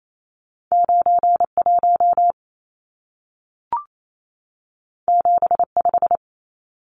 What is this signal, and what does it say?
Morse code, heard as a keyed tone, about 28 wpm, 700 Hz. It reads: 91 75
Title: Numbers - 2-Digit 28wpm